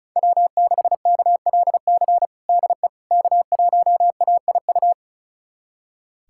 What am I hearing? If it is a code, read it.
Morse code, heard as a keyed tone, about 36 wpm, 700 Hz. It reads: W6KLC DE K1AIU
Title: USA Callsign DE Callsign - 5 Characters - 35wpm